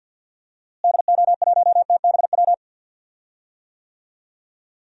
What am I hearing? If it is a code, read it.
Morse code, heard as a keyed tone, about 47 wpm, 700 Hz. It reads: DO1TBW
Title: DXCC-100 Call Signs - 6 Characters - Repeated 2x - 50wpm